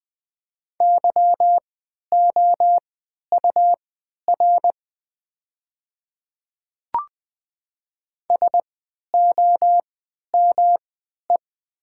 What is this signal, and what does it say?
Morse code, heard as a keyed tone, about 20 wpm, spaced as for 11 wpm, 700 Hz. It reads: YOUR SOME
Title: 3x Character Spacing - Top 100 Words - 20wpm